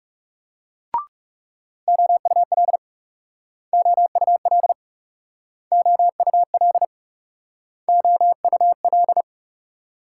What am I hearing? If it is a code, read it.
Morse code, heard as a keyed tone, about 37 wpm, 700 Hz. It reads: OUL OUL OUL OUL
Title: Speed-Racing Multi-letter Phonemes - 30wpm